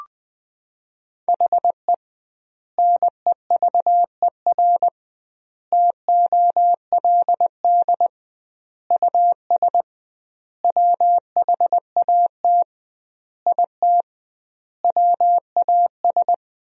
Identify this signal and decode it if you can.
Morse code, heard as a keyed tone, about 20 wpm, 700 Hz. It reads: HE NEVER TOLD US WHAT IT WAS
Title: 2x Word-Spacing - Sentences from Top 400 Words - 20wpm